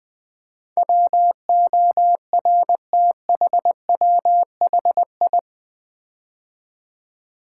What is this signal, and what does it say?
Morse code, heard as a keyed tone, about 20 wpm, 700 Hz. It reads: WORTHWHI
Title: Build A Word - 4 Letters At A Time - 10-Letter Words - 20wpm